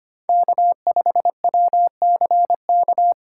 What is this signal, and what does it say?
Morse code, heard as a keyed tone, about 25 wpm, 700 Hz. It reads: K5WCK
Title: USA Call Signs - 5 Characters - 4x Character Spacing - 25wpm